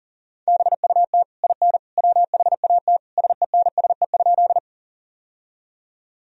Morse code, heard as a keyed tone, about 40 wpm, 700 Hz. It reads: BUT IN WHAT SENSE?
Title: Sentences from Top 1000 Words - Review Part 4 - 40wpm